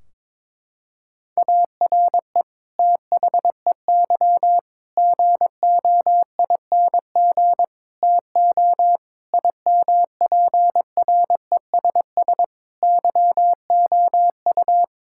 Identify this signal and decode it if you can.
Morse code, heard as a keyed tone, about 22 wpm, 700 Hz. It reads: ARE THEY GOING TO IMPRESS YOU
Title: Sentences from Top 1600 Words 22wpm